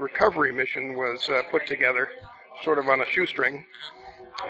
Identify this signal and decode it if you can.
Emergency Alert System (SAME) tones, heard as harsh buzzing bursts, with speech. It is heard from 0.5 to 3.6 s.